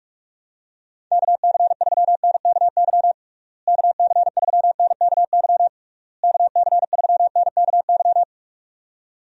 Morse code, heard as a keyed tone, about 41 wpm, 700 Hz. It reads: KC3NKY KC3NKY KC3NKY
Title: USA Call Signs - 6 Characters - Repeated 3x - 45wpm